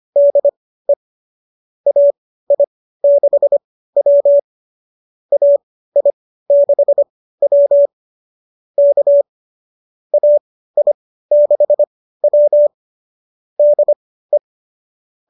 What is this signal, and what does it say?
Morse code, heard as a keyed tone, about 25 wpm, spaced as for 15 wpm, 590 Hz. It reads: DE AI6W AI6W K AI6W DE
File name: qso207 15